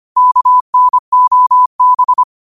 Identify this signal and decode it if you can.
Morse code, heard as a keyed tone, about 25 wpm, 1000 Hz. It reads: KNOB